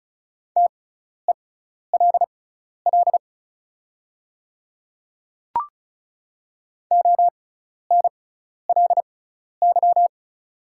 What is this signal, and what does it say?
Morse code, heard as a keyed tone, about 36 wpm, spaced as for 12 wpm, 690 Hz. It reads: TELL ONLY
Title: 6x Character Spacing - Top 200 Words - 35wpm